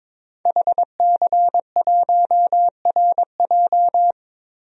Morse code, heard as a keyed tone, about 22 wpm, 700 Hz. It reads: HC1RJ
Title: DXCC-100 Call Signs - 5 Characters - Repeated 3x - 22wpm